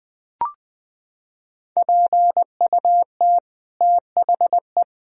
Morse code, heard as a keyed tone, about 20 wpm, 700 Hz. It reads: PUT THE
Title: Sentences from Top 1000 Words - Review Part 3 - 20wpm